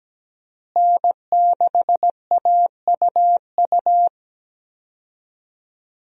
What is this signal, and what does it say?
Morse code, heard as a keyed tone, about 17 wpm, 700 Hz. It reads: N6AUU
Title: USA Call Signs - 5 Characters - Repeated 3x - 17wpm